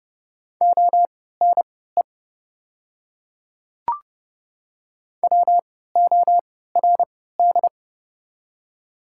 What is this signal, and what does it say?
Morse code, heard as a keyed tone, about 30 wpm, spaced as for 17 wpm, 690 Hz. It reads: ONE WORD
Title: 3x Character Spacing - Top 200 Words - 30wpm